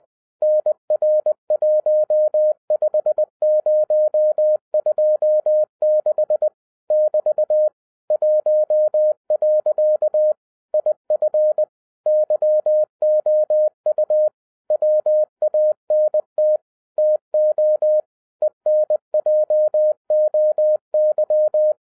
Morse code, heard as a keyed tone, about 20 wpm, 610 Hz. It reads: NR15026 <BT> 1. IF YOU WANT TO ENJOY
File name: 2011-10-mp3 / n13-TIPS FROM MY 34 YEARS OF EXPERIENCE 20wpm